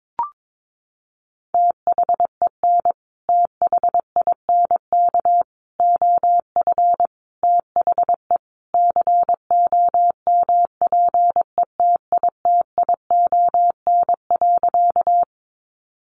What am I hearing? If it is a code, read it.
Morse code, heard as a keyed tone, about 22 wpm, 700 Hz. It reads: THEN THINK OF THE COMPETITION.
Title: Sentences from Top 1000 Words 22wpm